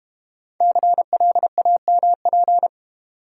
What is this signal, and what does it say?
Morse code, heard as a keyed tone, about 30 wpm, 700 Hz. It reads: CLAMP
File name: practice / 25wpm / Difficult Words - Speed-Racing with Hints - 25wpm